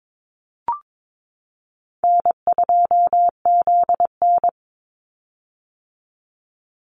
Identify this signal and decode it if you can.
Morse code, heard as a keyed tone, about 22 wpm, 700 Hz. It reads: N2ZN
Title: Easy Call Signs - 22wpm